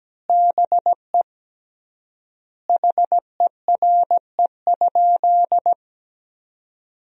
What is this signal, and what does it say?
Morse code, heard as a keyed tone, about 17 wpm, 700 Hz. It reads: BE HERE?
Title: 3x Word-Spacing - Sentences from Top 400 Words - 17wpm